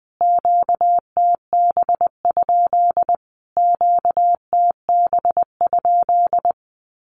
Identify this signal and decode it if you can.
Morse code, heard as a keyed tone, about 19 wpm, 700 Hz. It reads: QTB? QTB?
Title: Q-Codes Repeated 3x 20wpm